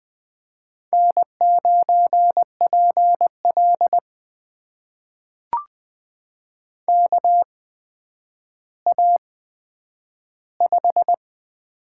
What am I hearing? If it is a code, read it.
Morse code, heard as a keyed tone, about 20 wpm, spaced as for 10 wpm, 700 Hz. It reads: N9PL KA5
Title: USA Call Signs - 8x Character Spacing - 20wpm